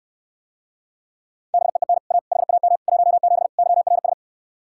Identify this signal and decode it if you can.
Morse code, heard as a keyed tone, about 69 wpm, 700 Hz. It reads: BEEN A HAM 17 YRS